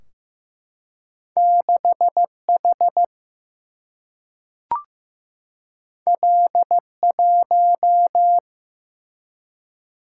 Morse code, heard as a keyed tone, about 15 wpm, 700 Hz. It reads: 6H L1
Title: Single-Letter Single-Number 15wpm